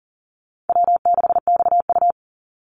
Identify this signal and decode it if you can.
Morse code, heard as a keyed tone, about 40 wpm, 700 Hz. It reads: W6XU
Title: Easy Call Signs - 40wpm